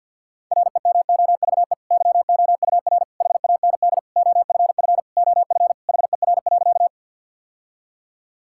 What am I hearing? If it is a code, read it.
Morse code, heard as a keyed tone, about 52 wpm, 700 Hz. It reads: REMOVE YOUR HAND OFF OF HER.
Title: Sentences from Top 1000 Words - Review Part 5 - 50wpm